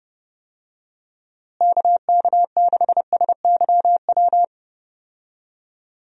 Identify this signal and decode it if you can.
Morse code, heard as a keyed tone, about 32 wpm, 700 Hz. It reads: KK6SYW
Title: USA Call Signs - 6 Characters - Repeated 3x - 30wpm